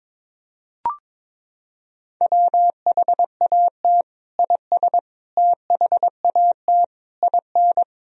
Morse code, heard as a keyed tone, about 22 wpm, 700 Hz. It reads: WHAT IS THAT IN
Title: Sentences from Top 200 Words and Abbreviations 22wpm